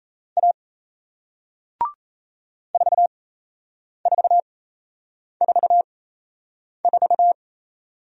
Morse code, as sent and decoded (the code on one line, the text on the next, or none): .- / ....- ....- ....- ....-
A 4444